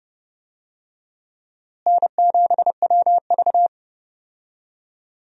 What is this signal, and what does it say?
Morse code, heard as a keyed tone, about 32 wpm, 700 Hz. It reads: N7WV
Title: USA Call Signs - 4 Characters - Repeated 2x - 30wpm